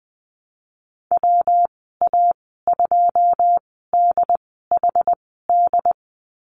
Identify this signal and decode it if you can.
Morse code, heard as a keyed tone, about 19 wpm, spaced as for 14 wpm, 700 Hz. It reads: WA2DHD
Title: 154 Lesson 25g - Letter p - Call Signs 20wpm